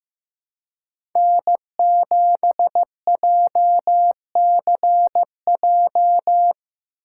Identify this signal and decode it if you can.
Morse code, heard as a keyed tone, about 18 wpm, 700 Hz. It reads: N7JCJ
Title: Speed Racing - USA 1x3 Call Signs 15wpm